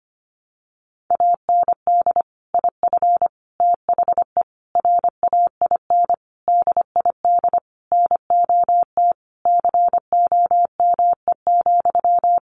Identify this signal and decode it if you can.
Morse code, heard as a keyed tone, about 25 wpm, 700 Hz. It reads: AND IF THE RAIN DID NOT COME,
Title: Sentences from Top 1300 Words 25wpm